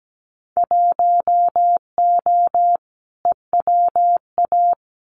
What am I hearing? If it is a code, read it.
Morse code, heard as a keyed tone, about 17 wpm, 700 Hz. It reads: 1O EWA
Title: Winter Field Day Exchanges Encore 17wpm